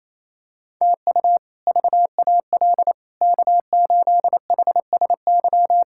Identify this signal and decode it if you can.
Morse code, heard as a keyed tone, about 28 wpm, 700 Hz. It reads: TU VAL K8HSY
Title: K1USN SST QSO Practice - 28wpm